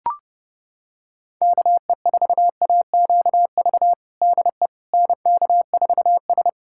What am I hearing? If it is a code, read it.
Morse code, heard as a keyed tone, about 31 wpm, 700 Hz. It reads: KE4AQV DE NK4S
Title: USA Callsign DE Callsign 30wpm